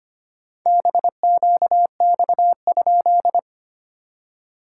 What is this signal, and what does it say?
Morse code, heard as a keyed tone, about 25 wpm, 690 Hz. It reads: BQX?
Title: SOTA QSO Practice 25wpm